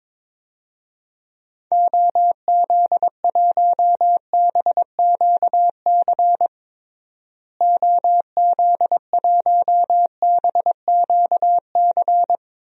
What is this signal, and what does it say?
Morse code, heard as a keyed tone, about 22 wpm, 700 Hz. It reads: OZ1BQC OZ1BQC
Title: DXCC-100 Call Signs - 6 Characters - Repeated 2x - 22wpm